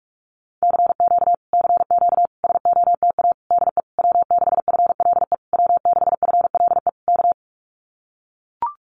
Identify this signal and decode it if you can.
Morse code, heard as a keyed tone, about 44 wpm, 700 Hz. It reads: CQ CQ SOTA DE W6FLE W6FLE K
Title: Common QSO Phrases Encore 45wpm